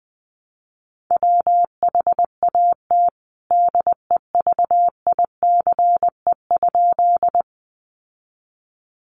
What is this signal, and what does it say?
Morse code, heard as a keyed tone, about 20 wpm, 700 Hz. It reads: WHAT DEVICE?
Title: Sentences from Top 1500 Words 20wpm